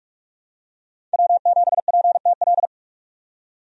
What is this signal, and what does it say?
Morse code, heard as a keyed tone, about 46 wpm, 700 Hz. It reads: W7PTL
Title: USA Call Signs - 5 Characters - 45wpm